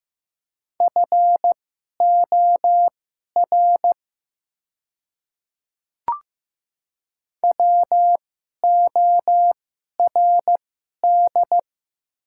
Morse code, heard as a keyed tone, about 15 wpm, spaced as for 11 wpm, 700 Hz. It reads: FOR WORD
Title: 2x Character Spacing - Top 100 Words - 15wpm